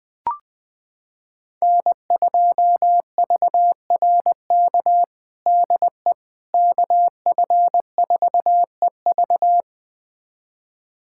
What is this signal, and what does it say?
Morse code, heard as a keyed tone, about 20 wpm, 700 Hz. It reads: N2VRK DE KF4EV
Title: USA Callsign DE Callsign - 5 Characters - 20wpm